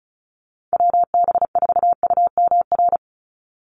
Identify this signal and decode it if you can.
Morse code, heard as a keyed tone, about 34 wpm, 700 Hz. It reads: WB4UMR
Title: USA Call Signs 35wpm